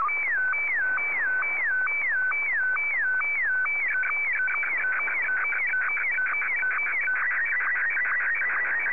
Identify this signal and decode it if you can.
SSTV (slow-scan television), Martin 1.
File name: sstv / M1-VK3AOB